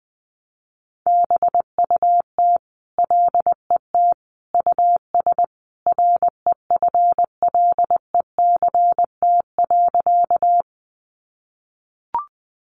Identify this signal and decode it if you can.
Morse code, heard as a keyed tone, about 20 wpm, 700 Hz. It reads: BUT LET US REFLECT.
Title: Sentences from Top 1500 Words 20wpm